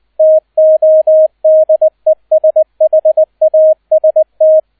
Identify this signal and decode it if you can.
Morse code, heard as a keyed tone, about 20 wpm, 630 Hz, with radio noise. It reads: TODESHAST